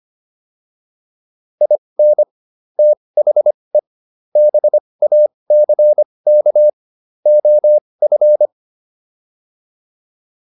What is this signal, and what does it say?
Morse code, heard as a keyed tone, about 25 wpm, spaced as for 20 wpm, 600 Hz. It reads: IN THE BACK OF
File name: WD304 20